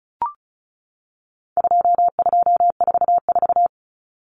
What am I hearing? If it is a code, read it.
Morse code, heard as a keyed tone, about 34 wpm, 700 Hz. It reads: 2244
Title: Common Cut Numbers - 4-Digit 35wpm